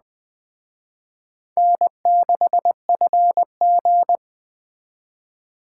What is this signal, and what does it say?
Morse code, heard as a keyed tone, about 21 wpm, 700 Hz. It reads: N6FG